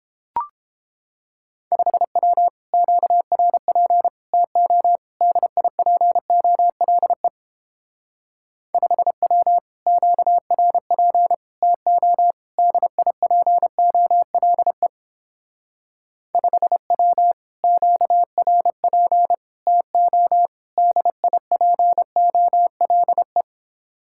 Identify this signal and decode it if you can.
Morse code, heard as a keyed tone, about 29 wpm, 700 Hz. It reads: 5W QRP TO DIPOLE 5W QRP TO DIPOLE 5W QRP TO DIPOLE